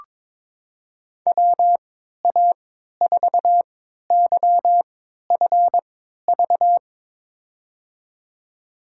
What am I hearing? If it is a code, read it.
Morse code, heard as a keyed tone, about 22 wpm, spaced as for 12 wpm, 700 Hz. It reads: WA4YFV
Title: USA Call Signs - 6 Characters - 3x Character Spacing - 22wpm